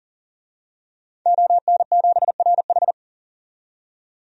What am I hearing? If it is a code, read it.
Morse code, heard as a keyed tone, about 40 wpm, 700 Hz. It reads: ON7RH
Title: DXCC-100 Call Signs - 5 Characters - 40wpm